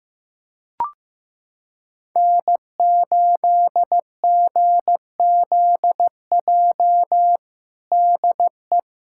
Morse code, heard as a keyed tone, about 15 wpm, 700 Hz. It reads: N8GZJ DE
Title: Practice QSO - Exchange 4 - Ending - 15wpm